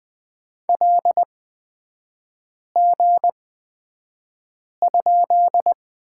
Morse code, heard as a keyed tone, about 20 wpm, spaced as for 5 wpm, 700 Hz. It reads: LG?
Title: 224 Lesson 36e - BT - Three Characters 20wpm